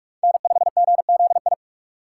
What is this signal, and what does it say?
Morse code, heard as a keyed tone, about 45 wpm, 700 Hz. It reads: N5GZI
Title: USA Call Signs - 4x Character Spacing - 45wpm